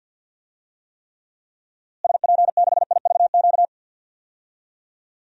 Morse code, heard as a keyed tone, about 47 wpm, 700 Hz. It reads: SP6IVQ
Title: DXCC-100 Call Signs - 6 Characters - Repeated 2x - 50wpm